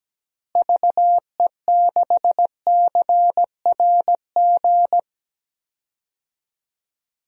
Morse code, heard as a keyed tone, about 17 wpm, 700 Hz. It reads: VE6CRG
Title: DXCC-100 Call Signs - 6 Characters - Repeated 3x - 17wpm